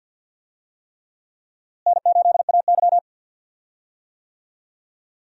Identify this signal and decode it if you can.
Morse code, heard as a keyed tone, about 53 wpm, 700 Hz. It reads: N8AY
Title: USA Call Signs - 4 Characters - Repeated 3x - 50wpm